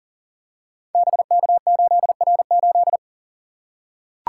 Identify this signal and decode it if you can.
Morse code, heard as a keyed tone, about 38 wpm, 700 Hz. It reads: BK8R8
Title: Code Groups - 5 Letter:Numbers - 40wpm